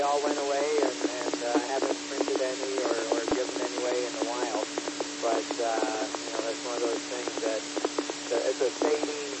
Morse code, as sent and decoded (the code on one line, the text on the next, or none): none